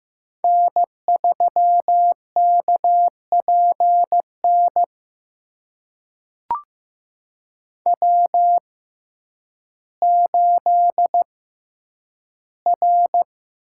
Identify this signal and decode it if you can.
Morse code, heard as a keyed tone, about 15 wpm, spaced as for 9 wpm, 700 Hz. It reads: N3KPN W8R
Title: USA Call Signs - 6x Character Spacing - 15wpm